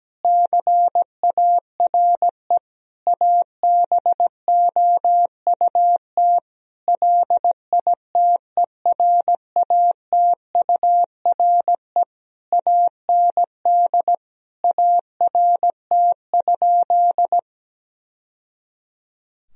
Morse code, heard as a keyed tone, about 17 wpm, 700 Hz. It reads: CARE ABOUT LITERATURE AND ART?
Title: Sentences from Top 1000 Words - Review Part 3 - 17wpm